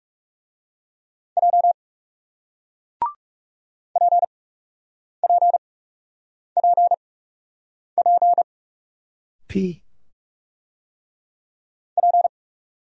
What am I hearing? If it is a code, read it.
Morse code, heard as a keyed tone, about 38 wpm, spaced as for 8 wpm, 700 Hz. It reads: J PPPP P